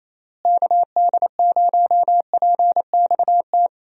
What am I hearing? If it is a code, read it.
Morse code, heard as a keyed tone, about 28 wpm, 700 Hz. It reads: KD0PXT
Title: USA Call Signs - 6 Characters - Repeated 2x - 28wpm